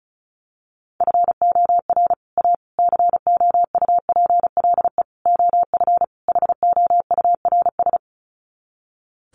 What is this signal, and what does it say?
Morse code, heard as a keyed tone, about 34 wpm, 700 Hz. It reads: FOR A COUPLE OF HOURS
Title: Sets of 5 Words Encore2 35wpm